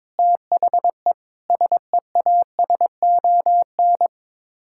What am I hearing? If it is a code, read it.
Morse code, heard as a keyed tone, about 22 wpm, 700 Hz. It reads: THE SEASON
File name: practice / 22wpm / Sets of 6 Words - 22wpm